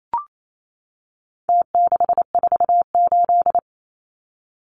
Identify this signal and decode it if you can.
Morse code, heard as a keyed tone, about 28 wpm, 700 Hz. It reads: T648
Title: Numbers - Contest Style - 4-Digit 28wpm